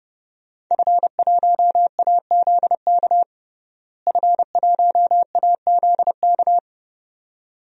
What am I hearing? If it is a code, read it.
Morse code, heard as a keyed tone, about 30 wpm, 700 Hz. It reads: F1AZK F1AZK
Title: DXCC-100 Call Signs - 5 Characters - Repeated 2x - 30wpm